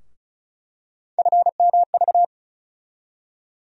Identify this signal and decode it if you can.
Morse code, heard as a keyed tone, about 36 wpm, 700 Hz. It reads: FMV
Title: USA Call Sign Postfixes - 3 Letters 35wpm